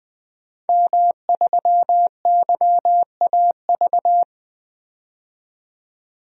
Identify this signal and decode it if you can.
Morse code, heard as a keyed tone, about 20 wpm, 700 Hz. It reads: M3YAV
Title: DXCC-100 Call Signs - 5 Characters - 20wpm